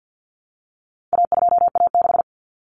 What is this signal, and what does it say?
Morse code, heard as a keyed tone, about 50 wpm, 700 Hz. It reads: A2A6
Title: Common Cut Numbers - 4-Digit 50wpm